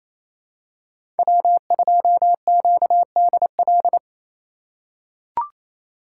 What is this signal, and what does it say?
Morse code, heard as a keyed tone, about 28 wpm, 700 Hz. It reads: W2QDL
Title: USA Call Signs - 4x Character Spacing - 28wpm